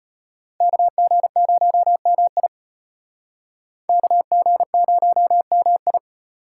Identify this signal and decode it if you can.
Morse code, heard as a keyed tone, about 32 wpm, 700 Hz. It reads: KG0MI KG0MI